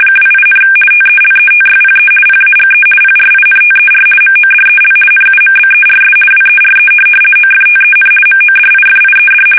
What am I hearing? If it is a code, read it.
RTTY, about 50 baud. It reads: 28, TELETYPE PAGE PRINTER MODEL NUMBERS WERE OFTEN MODIFIED
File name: fsk850-50B / teletype-corp